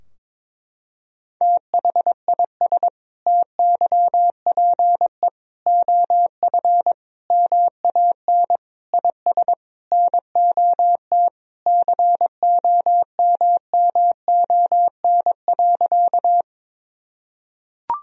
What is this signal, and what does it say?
Morse code, heard as a keyed tone, about 22 wpm, 700 Hz. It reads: THIS TYPE OF MAN IS NOT COMMON.
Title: Sentences from Top 1000 Words - Review Part 3 - 22wpm